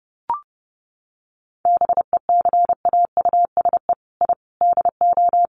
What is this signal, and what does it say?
Morse code, heard as a keyed tone, about 30 wpm, 700 Hz. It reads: BECAUSE I DO
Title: Sets of 4 Words 30wpm